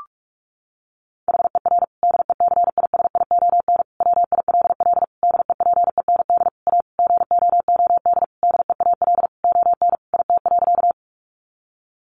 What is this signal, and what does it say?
Morse code, heard as a keyed tone, about 44 wpm, 700 Hz. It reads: HER DECISION WILL DEPEND A GOOD DEAL ON IT.
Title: Sentences from Top 1300 Words 45wpm